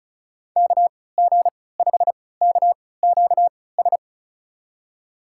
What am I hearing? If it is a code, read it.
Morse code, heard as a keyed tone, about 34 wpm, spaced as for 20 wpm, 700 Hz. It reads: KG5KQS
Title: USA Call Signs - 6 Characters - 3x Character Spacing - 35wpm